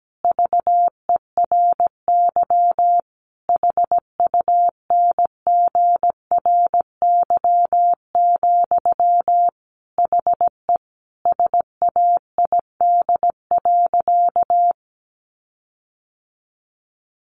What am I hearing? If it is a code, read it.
Morse code, heard as a keyed tone, about 17 wpm, 700 Hz. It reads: VERY HUNGRY, HE SAID.
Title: Sentences from Top 1400 Words 17wpm